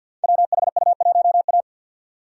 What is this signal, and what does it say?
Morse code, heard as a keyed tone, about 52 wpm, 700 Hz. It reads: WHU1A
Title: Code Groups - 5 Letter:Numbers - 50wpm